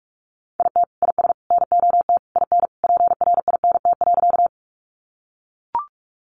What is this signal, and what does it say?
Morse code, heard as a keyed tone, about 44 wpm, 700 Hz. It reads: IT IS NOT IN PRINT.
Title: Sentences from Top 1400 Words 45wpm